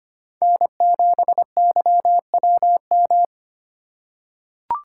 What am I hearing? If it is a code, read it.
Morse code, heard as a keyed tone, about 25 wpm, 700 Hz. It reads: N7YWM
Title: USA Call Signs - 5 Characters - 4x Character Spacing - 25wpm